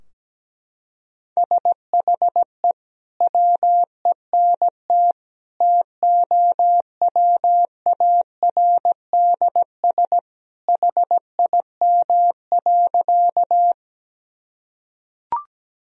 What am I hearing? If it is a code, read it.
Morse code, heard as a keyed tone, about 17 wpm, 700 Hz. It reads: SHE WENT TOWARDS HIM.
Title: Sentences from Top 1000 Words - Review Part 2 - 17wpm